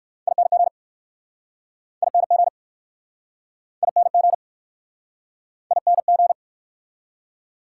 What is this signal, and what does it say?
Morse code, heard as a keyed tone, about 60 wpm, 690 Hz. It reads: ING ING ING ING